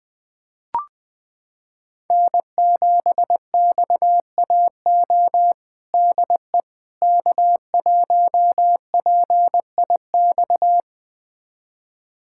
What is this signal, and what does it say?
Morse code, heard as a keyed tone, about 20 wpm, 700 Hz. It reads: N7XAO DE K1PIX
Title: USA Callsign DE Callsign - 5 Characters - 20wpm